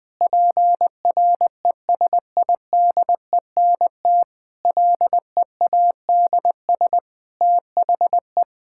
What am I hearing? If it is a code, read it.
Morse code, heard as a keyed tone, about 21 wpm, 700 Hz. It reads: PRESIDENT LEADS THE
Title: Sentences with 5-Word Phrases - 20wpm